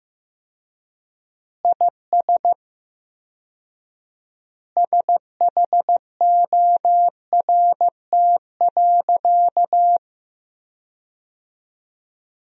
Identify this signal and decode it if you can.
Morse code, heard as a keyed tone, about 15 wpm, 700 Hz. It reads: IS SHORT.
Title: 4x Word-Spacing - Sentences from Top 500 Words - 15wpm